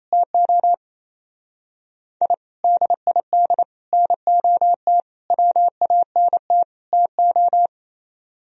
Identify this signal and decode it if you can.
Morse code, heard as a keyed tone, about 34 wpm, 700 Hz. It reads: TO I DID NOT WANT TO
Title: Speed-Racing - Sets of 5 Words 28wpm